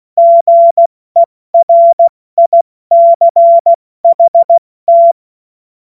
Morse code, heard as a keyed tone, about 16 wpm, spaced as for 14 wpm, 680 Hz. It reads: GERICHT
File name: Morsekurs / Witze / Witz 31. Mai 2022 70 F80